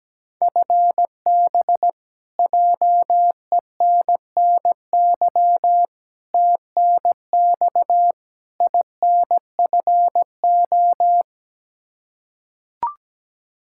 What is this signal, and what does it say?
Morse code, heard as a keyed tone, about 17 wpm, 700 Hz. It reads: FB JENNY TNX INFO
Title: Practice QSO - Exchange 3 - More Info - 17wpm